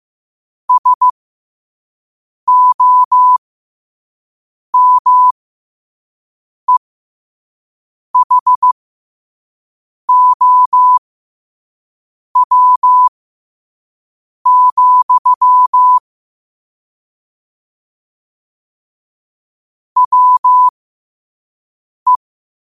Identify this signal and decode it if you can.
Morse code, heard as a keyed tone, about 15 wpm, spaced as for 5 wpm, 1000 Hz. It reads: SOMEHOW, WE
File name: qotdfiles / 5wpm